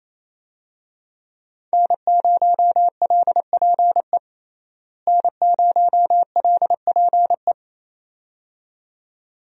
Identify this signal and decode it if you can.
Morse code, heard as a keyed tone, about 28 wpm, 700 Hz. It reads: N0LPE N0LPE
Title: USA Call Signs - 5 Characters - Repeated 2x - 28wpm